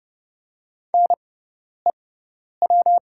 Morse code, heard as a keyed tone, about 29 wpm, spaced as for 10 wpm, 700 Hz. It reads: NEW
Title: 6x Character Spacing - Top 500 Words - 30wpm